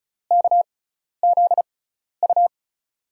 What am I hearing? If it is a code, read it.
Morse code, heard as a keyed tone, about 36 wpm, spaced as for 12 wpm, 700 Hz. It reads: KZU
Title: USA Call Signs - 6x Character Spacing - 35wpm